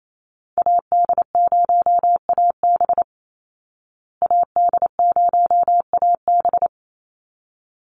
Morse code, heard as a keyed tone, about 28 wpm, 700 Hz. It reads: AD0AB AD0AB
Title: Call Signs Repeated 3x - 28wpm